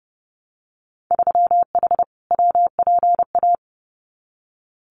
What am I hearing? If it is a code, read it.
Morse code, heard as a keyed tone, about 30 wpm, 700 Hz. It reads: 3H WPA
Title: Winter Field Day Exchanges Encore 30wpm